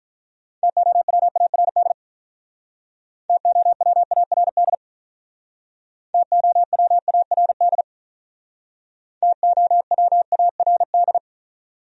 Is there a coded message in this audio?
Yes: Morse code from 0.6 to 11.2 s, heard as a keyed tone, about 47 wpm, 700 Hz. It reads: TOWARD TOWARD TOWARD TOWARD